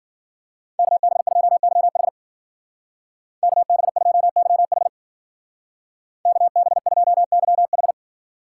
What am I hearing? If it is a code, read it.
Morse code, heard as a keyed tone, about 50 wpm, 700 Hz. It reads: KB2YH KB2YH KB2YH